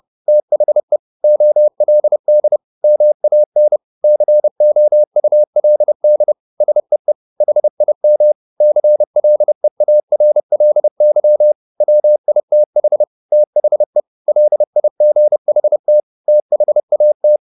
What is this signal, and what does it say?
Morse code, heard as a keyed tone, about 30 wpm, 600 Hz. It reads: THE OLD MAN COULD SEE HIM CLEARLY WITH THE LIGHT THAT
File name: the old man and the sea / the old man and the sea 1 30wpm